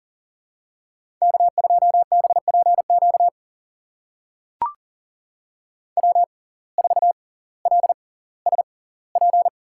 Morse code, heard as a keyed tone, about 38 wpm, spaced as for 20 wpm, 700 Hz. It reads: K2BPQ W4LSP
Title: USA Call Signs - 5 Characters - 6x Character Spacing - 40wpm